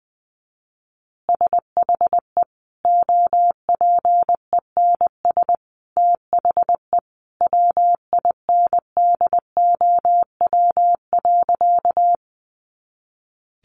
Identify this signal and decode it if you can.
Morse code, heard as a keyed tone, about 20 wpm, 700 Hz. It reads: SHE OPENS THE WINDOW.
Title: Additional Easy Phrases 20wpm